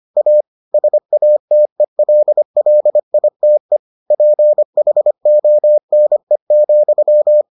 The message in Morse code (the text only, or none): A SATELLITE PHONE,